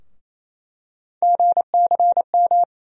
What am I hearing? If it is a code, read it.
Morse code, heard as a keyed tone, about 27 wpm, 700 Hz. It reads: GCM
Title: USA Call Sign Postfixes - 3 Letters 28wpm